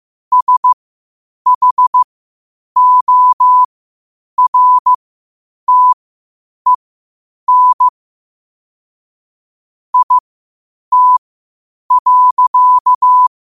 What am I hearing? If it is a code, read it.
Morse code, heard as a keyed tone, about 15 wpm, spaced as for 8 wpm, 1000 Hz. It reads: SHORTEN IT.
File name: qotdfiles / 8wpm